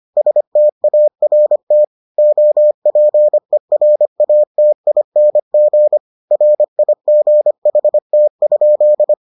Morse code, heard as a keyed tone, about 25 wpm, 600 Hz. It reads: START OPERATING RIGHT?